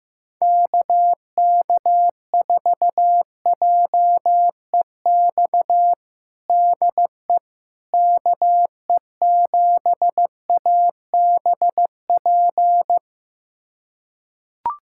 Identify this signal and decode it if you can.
Morse code, heard as a keyed tone, about 15 wpm, 700 Hz. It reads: KK4JEX DE KE7ABP
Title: USA Callsign DE Callsign - 6 Characters - 15wpm